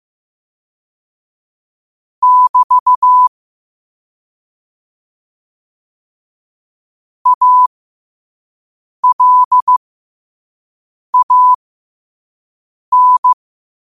Morse code, heard as a keyed tone, about 15 wpm, spaced as for 5 wpm, 1000 Hz. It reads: <BT> ALAN
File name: qotdfiles / 5wpm